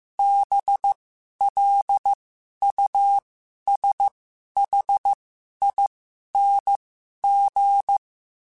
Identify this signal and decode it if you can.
Morse code, heard as a keyed tone, about 15 wpm, spaced as for 11 wpm, 800 Hz. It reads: BLUSHING
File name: Special INGs-mp3s / Special INGs-mp3s / Sngl Ing @10